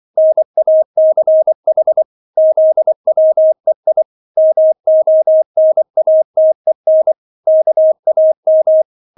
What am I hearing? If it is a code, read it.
Morse code, heard as a keyed tone, about 24 wpm, 630 Hz. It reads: NACH ZWEI MONATEN KAM